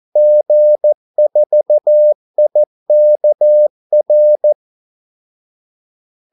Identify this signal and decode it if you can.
Morse code, heard as a keyed tone, about 14 wpm, 600 Hz. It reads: G4IKR
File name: Callsigns-14wpm